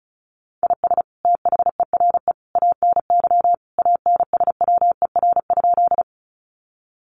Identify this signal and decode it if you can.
Morse code, heard as a keyed tone, about 34 wpm, 700 Hz. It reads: IS THERE ANY ANSWER?
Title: Sentences from Top 300 Words 35wpm